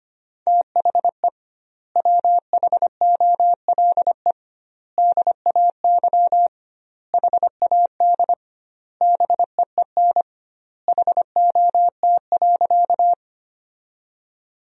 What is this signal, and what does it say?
Morse code, heard as a keyed tone, about 25 wpm, 700 Hz. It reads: THE WHOLE DAY HAD BEEN HOT.
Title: 2x Word-Spacing - Sentences from Top 500 Words - 25wpm